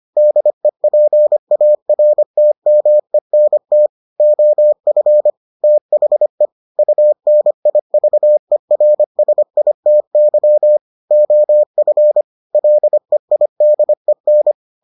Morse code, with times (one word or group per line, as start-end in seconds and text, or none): 0.2-3.9 DEPARTMENT
4.2-5.3 OF
5.6-6.5 THE
6.8-10.8 UNIVERSITY
11.1-12.2 OF
12.5-14.5 LEIDEN